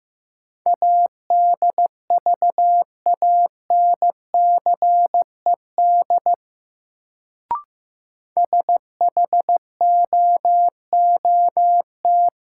Morse code, heard as a keyed tone, about 15 wpm, 700 Hz. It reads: ADVANCED SHOOT
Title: Sentences from Top 1200 Words - New Word Drill Alt - 15wpm